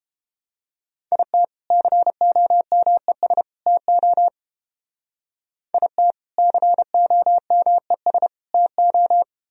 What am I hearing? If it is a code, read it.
Morse code, heard as a keyed tone, about 31 wpm, 700 Hz. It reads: IT COMES TO IT COMES TO